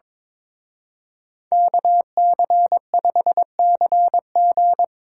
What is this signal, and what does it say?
Morse code, heard as a keyed tone, about 24 wpm, 700 Hz. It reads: KC5CG
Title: Speed Racing - USA 2x2 Call Signs 22wpm